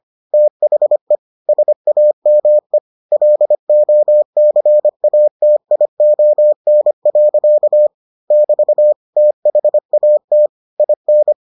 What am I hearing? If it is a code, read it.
Morse code, heard as a keyed tone, about 25 wpm, 600 Hz. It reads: THE SAME LOCATION. <BT> THAT IN